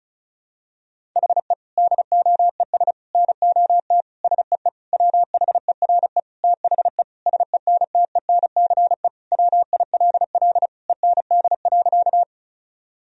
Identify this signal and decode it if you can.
Morse code, heard as a keyed tone, about 34 wpm, 700 Hz. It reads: HE DOES NOT SEE WHERE THE SENTENCE WILL END.
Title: Sentences from Top 1000 Words - Review Part 1 - 35wpm